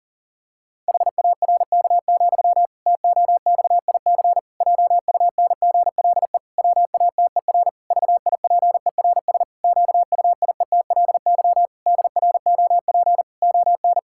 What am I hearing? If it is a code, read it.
Morse code, heard as a keyed tone, about 40 wpm, 700 Hz. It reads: HARK, TOXIC JUNGLE WATER VIPERS QUIETLY DROP ON
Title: Pangram Challenge v2 - 40wpm